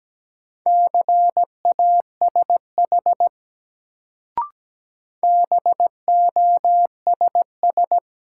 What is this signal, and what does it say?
Morse code, heard as a keyed tone, about 17 wpm, 700 Hz. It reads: CASH BOSS
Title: Sentences from Top 700 Words - New Word Drill - 17wpm